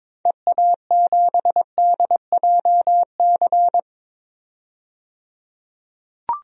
Morse code, heard as a keyed tone, about 22 wpm, 700 Hz. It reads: EA7DJC
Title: DXCC-100 Call Signs - 6 Characters - 22wpm